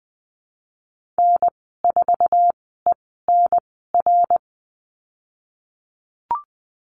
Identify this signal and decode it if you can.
Morse code, heard as a keyed tone, about 20 wpm, spaced as for 15 wpm, 700 Hz. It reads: N4ENR
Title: 055 Lesson 10g - Letter r - Call Signs 20wpm